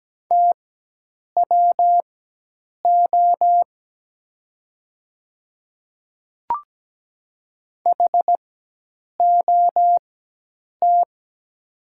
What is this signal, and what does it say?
Morse code, heard as a keyed tone, about 17 wpm, spaced as for 8 wpm, 700 Hz. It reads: TWO HOT
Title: 4x Character Spacing - Top 100 Words - 17wpm